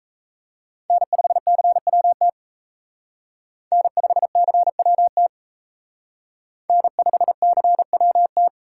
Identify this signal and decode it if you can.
Morse code, heard as a keyed tone, about 32 wpm, 700 Hz. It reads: N5CWT N5CWT N5CWT
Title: Speed Racing - USA 1x3 Call Signs 28wpm